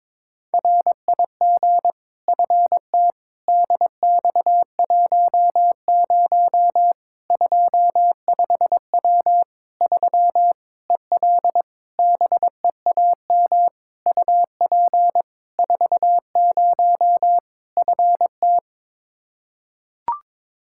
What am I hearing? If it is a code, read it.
Morse code, heard as a keyed tone, about 22 wpm, 700 Hz. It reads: RIG FT DX10 25W 3 EL BEAM UP 40 FT
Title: Practice QSO - Exchange 2 - Info - 22wpm